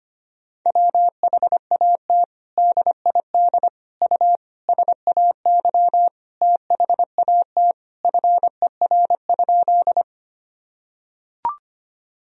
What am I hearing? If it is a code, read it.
Morse code, heard as a keyed tone, about 25 wpm, 700 Hz. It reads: WHAT DID U SAY THAT FER?
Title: Sentences from Top 200 Words and Abbreviations 25wpm